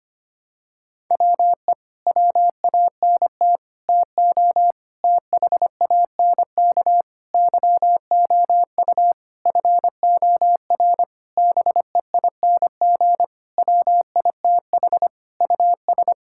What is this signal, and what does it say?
Morse code, heard as a keyed tone, about 25 wpm, 700 Hz. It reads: WE WANT TO THANK YOU FOR BEING WITH US
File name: practice / 25wpm / Sentences with 9-Word Phrases - 25wpm